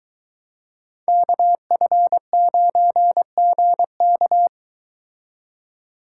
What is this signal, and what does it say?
Morse code, heard as a keyed tone, about 23 wpm, 700 Hz. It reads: KF9GK